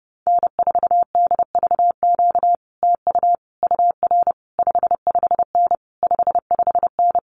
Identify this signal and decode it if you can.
Morse code, heard as a keyed tone, about 30 wpm, 700 Hz. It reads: N4DVQ TU UR 55N 55N
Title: Common QSO Phrases Encore 30wpm